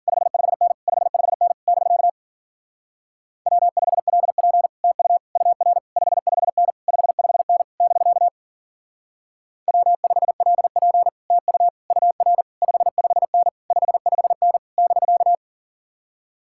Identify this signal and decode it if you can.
Morse code, heard as a keyed tone, about 52 wpm, 700 Hz. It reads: 55N 55N <BK> W5LP TU UR 55N 55N <BK> W5LP TU UR 55N 55N <BK>